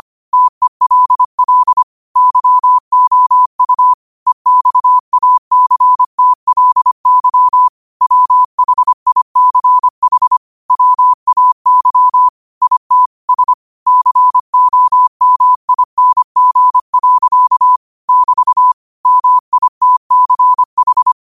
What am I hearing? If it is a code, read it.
Morse code, heard as a keyed tone, about 25 wpm, 1000 Hz. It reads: TELL YOU EXACTLY WHICH WAY IT S COMING. <BT> MITCH
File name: qotdfiles / 25wpm